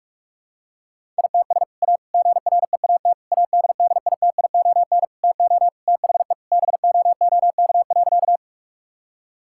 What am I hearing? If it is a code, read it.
Morse code, heard as a keyed tone, about 41 wpm, 700 Hz. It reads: ITS A GREAT ADDITION TO THE BOOK.
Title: Sentences from Top 1000 Words - Review Part 6 - 45wpm